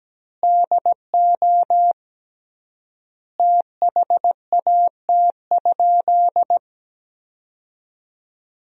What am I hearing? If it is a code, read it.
Morse code, heard as a keyed tone, about 17 wpm, 700 Hz. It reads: DO THAT?
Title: 3x Word-Spacing - Sentences from Top 100 Words - 17wpm